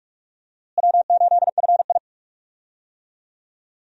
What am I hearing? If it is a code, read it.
Morse code, heard as a keyed tone, about 46 wpm, 700 Hz. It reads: W8FI
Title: USA Call Signs - 4 Characters - Repeated 3x - 45wpm